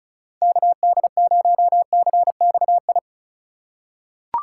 Morse code, heard as a keyed tone, about 34 wpm, 700 Hz. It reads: KD0CXI
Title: USA Call Signs - 6 Characters - 4x Character Spacing - 35wpm